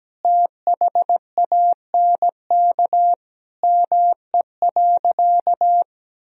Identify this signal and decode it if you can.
Morse code, heard as a keyed tone, about 17 wpm, 700 Hz. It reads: THANK ME.
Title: Sentences from Top 1000 Words - Review Part 1 - 17wpm